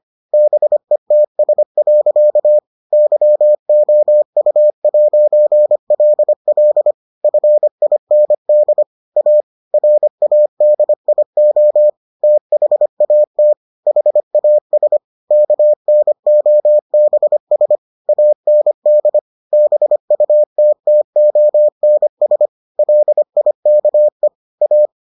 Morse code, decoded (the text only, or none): BETS. YOU'LL FIND A RADIO THAT HAS KNOBS AND BUTTONS LIKE A